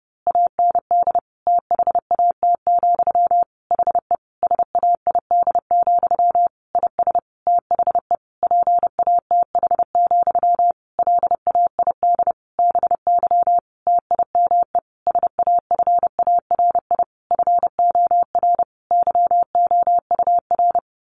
Morse code, heard as a keyed tone, about 30 wpm, 700 Hz. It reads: AND THAT, HE SAID, IS THE PATH, LAID BY TIME SAFARI FOR YOUR
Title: A Sound Of Thunder 30wpm